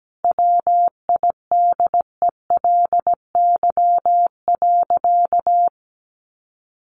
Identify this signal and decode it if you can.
Morse code, heard as a keyed tone, about 17 wpm, 700 Hz. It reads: WIDELY.
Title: Sentences from Top 1200 Words 17wpm